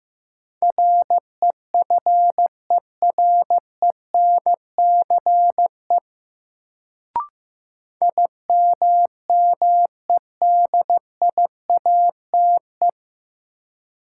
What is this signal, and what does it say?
Morse code, heard as a keyed tone, about 15 wpm, 700 Hz. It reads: REFERENCE IMMEDIATE
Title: Sentences from Top 1200 Words - New Word Drill Alt - 15wpm